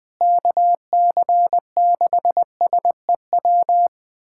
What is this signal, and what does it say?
Morse code, heard as a keyed tone, about 20 wpm, 700 Hz. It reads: KC6SEW
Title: USA Callsign DE Callsign 20wpm